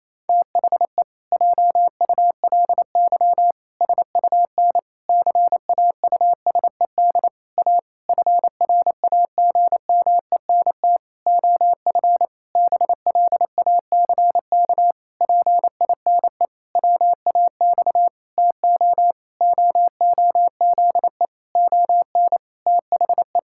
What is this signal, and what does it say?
Morse code, heard as a keyed tone, about 28 wpm, 700 Hz. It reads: THE JULY SUN CAUSED A FRAGMENT OF BLACK PINE WAX TO OOZE ON THE
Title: Pangram Challenge v2 - 28wpm